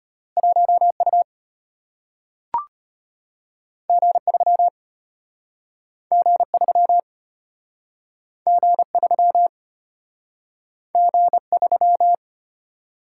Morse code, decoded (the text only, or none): 1U G3 G3 G3 G3